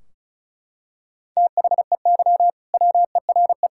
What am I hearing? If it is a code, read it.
Morse code, heard as a keyed tone, about 36 wpm, 700 Hz. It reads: THEY WERE
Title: Sentences from Top 1600 Words 35wpm